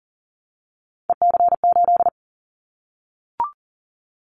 Morse code, heard as a keyed tone, about 42 wpm, 690 Hz. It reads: EC8
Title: Callsign Prefixes for Top 100 DXCC - 40wpm